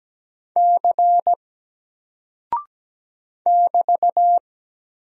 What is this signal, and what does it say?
Morse code, heard as a keyed tone, about 17 wpm, 700 Hz. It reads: C <BT>
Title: Single Character (Letter:Number:Punctuation:Prosign) Alt 17wpm